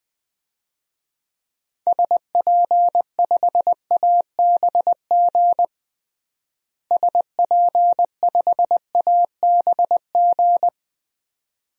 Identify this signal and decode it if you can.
Morse code, heard as a keyed tone, about 20 wpm, 700 Hz. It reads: SP5ABG SP5ABG
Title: DXCC-100 Call Signs - 6 Characters - Repeated 2x - 20wpm